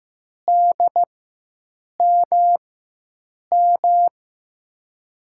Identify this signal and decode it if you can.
Morse code, heard as a keyed tone, about 16 wpm, spaced as for 7 wpm, 700 Hz. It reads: DMM
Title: USA Call Signs - 6 Characters - 4x Character Spacing - 15wpm